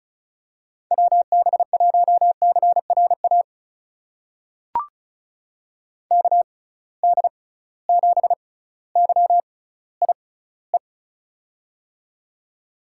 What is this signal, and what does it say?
Morse code, heard as a keyed tone, about 36 wpm, spaced as for 18 wpm, 700 Hz. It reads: WB1CRA KD7YIE
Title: USA Call Signs - 6 Characters - 6x Character Spacing - 35wpm